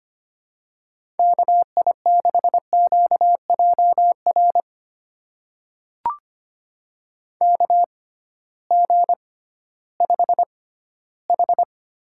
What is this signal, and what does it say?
Morse code, heard as a keyed tone, about 25 wpm, spaced as for 15 wpm, 700 Hz. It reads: KI6QJR KG5H
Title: USA Call Signs - 6 Characters - 6x Character Spacing - 25wpm